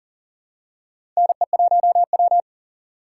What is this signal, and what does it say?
Morse code, heard as a keyed tone, about 40 wpm, 700 Hz. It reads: NE1W
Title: USA Call Signs - 4 Characters - 2x Character Spacing - 40wpm